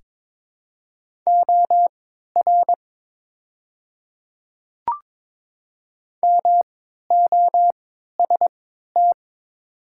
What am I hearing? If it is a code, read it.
Morse code, heard as a keyed tone, about 22 wpm, spaced as for 12 wpm, 700 Hz. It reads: OR MOST
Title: 3x Character Spacing - Top 100 Words - 22wpm